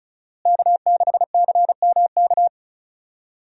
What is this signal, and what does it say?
Morse code, heard as a keyed tone, about 36 wpm, 700 Hz. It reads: K6CMK
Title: USA Call Signs - 5 Characters - Repeated 3x - 35wpm